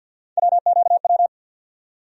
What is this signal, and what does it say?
Morse code, heard as a keyed tone, about 53 wpm, 700 Hz. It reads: WQW
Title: USA Call Sign Postfixes - 3 Letters 50wpm